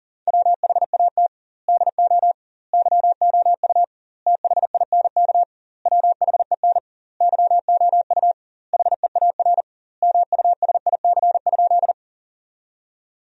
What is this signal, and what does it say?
Morse code, heard as a keyed tone, about 40 wpm, 700 Hz. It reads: WHAT DO YOU THINK WHEN YOU HEAR MUSIC?
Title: 2x Word-Spacing - Sentences from Top 300 Words - 40wpm